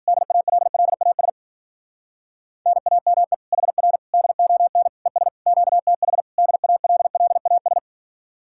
Morse code, heard as a keyed tone, about 52 wpm, 690 Hz. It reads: DALLAS NAME HR DON ES QTH DALLAS